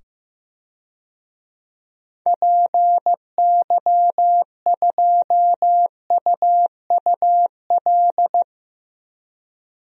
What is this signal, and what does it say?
Morse code, heard as a keyed tone, about 15 wpm, 700 Hz. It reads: PY2UUL
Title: DXCC-100 Call Signs - 6 Characters - 15wpm